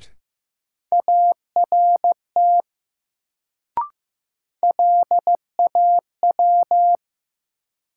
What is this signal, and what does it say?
Morse code, heard as a keyed tone, about 15 wpm, 700 Hz. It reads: ART LAW
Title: Sentences from Top 600 Words - New Word Drill - 15wpm